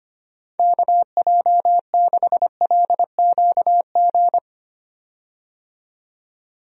Morse code, heard as a keyed tone, about 25 wpm, 700 Hz. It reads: KJ6LQG
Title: USA Call Signs - 6 Characters - 25wpm